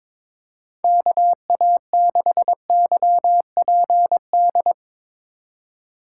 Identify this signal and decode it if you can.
Morse code, heard as a keyed tone, about 22 wpm, 700 Hz. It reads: KA6YPD
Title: USA Call Signs - 3x Character Spacing - 22wpm